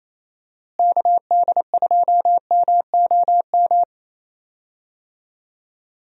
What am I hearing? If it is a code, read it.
Morse code, heard as a keyed tone, about 27 wpm, 700 Hz. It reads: KD2MOM
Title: USA Call Signs - 6 Characters - Repeated 3x - 28wpm